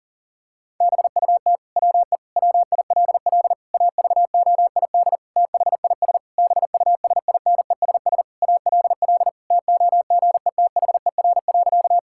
Morse code, heard as a keyed tone, about 37 wpm, 700 Hz. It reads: BUT WE WILL AVOID THIS BUSINESS ALL TOGETHER.
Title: Sentences from Top 1000 Words - Review Part 6 - 40wpm